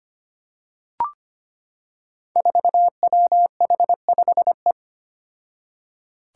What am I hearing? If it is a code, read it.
Morse code, heard as a keyed tone, about 25 wpm, 700 Hz. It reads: 4WH5E
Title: Five Challenging Letter-Number Combinations 25wpm